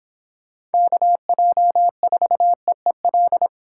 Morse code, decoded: KJ4EEL

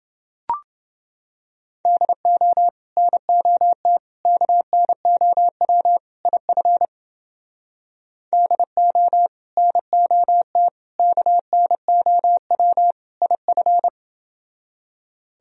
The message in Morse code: DO NOT KNOW IF DO NOT KNOW IF

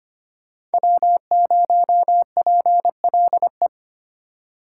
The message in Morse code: W0PLE

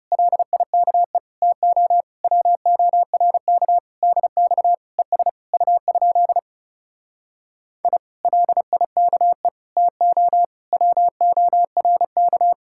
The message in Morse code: LIKE TO WORK DX ES U? I LIKE TO WORK

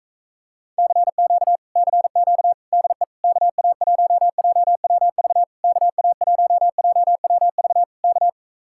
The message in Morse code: CQ CQ DE KA1JWV KA1JWV K

